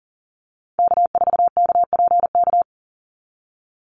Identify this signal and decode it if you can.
Morse code, heard as a keyed tone, about 38 wpm, 700 Hz. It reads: K4KPK
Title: NA SOTA Activator Callsigns 40wpm